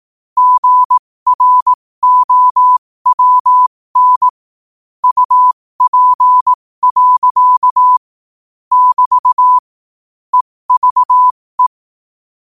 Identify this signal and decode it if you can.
Morse code, heard as a keyed tone, about 18 wpm, spaced as for 15 wpm, 1000 Hz. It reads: GROWN UP. <BT> EVE
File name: qotdfiles / 15wpm